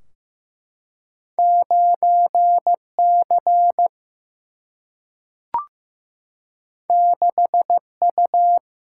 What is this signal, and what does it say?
Morse code, heard as a keyed tone, about 15 wpm, 700 Hz. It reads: 9C 6U
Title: Single-Letter Single-Number 15wpm